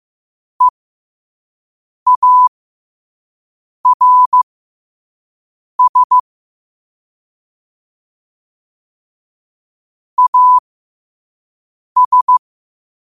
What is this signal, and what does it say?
Morse code, heard as a keyed tone, about 15 wpm, spaced as for 5 wpm, 1000 Hz. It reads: EARS AS
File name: qotdfiles / 5wpm